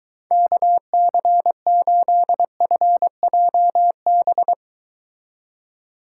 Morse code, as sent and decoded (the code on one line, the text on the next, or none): -.- -.-. ---.. ..-. .--- -...
KC8FJB